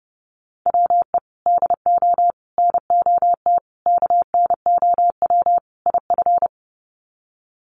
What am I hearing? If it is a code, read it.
Morse code, heard as a keyed tone, about 30 wpm, 700 Hz. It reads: WE DO NOT KNOW IF
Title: Sets of 5 Words Encore2 30wpm